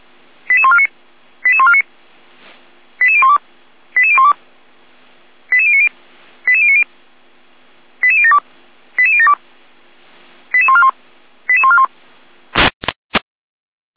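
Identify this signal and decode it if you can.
Five-tone selective call (ZVEI), heard as quick runs of five tones, tones of about 70 ms. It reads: 88149 88149 88012 88012 88090 88090 88072 88072 88141 88141
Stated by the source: ZVEI five-tone call 88147 received on SDR.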